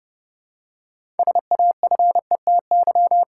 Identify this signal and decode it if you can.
Morse code, heard as a keyed tone, about 32 wpm, 700 Hz. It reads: SAFETY
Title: Six Letter Words - 30wpm